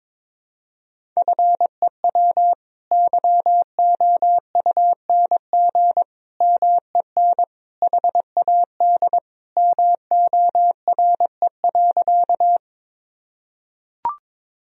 Morse code, heard as a keyed tone, about 21 wpm, 700 Hz. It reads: FEW YOUNG MEN HAD MORE.
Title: Sentences from Top 1000 Words - Review Part 1 - 22wpm